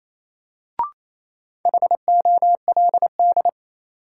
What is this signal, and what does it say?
Morse code, heard as a keyed tone, about 28 wpm, 700 Hz. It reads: HOLD
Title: Sentences from Top 400 Words - New Word Drill - 28wpm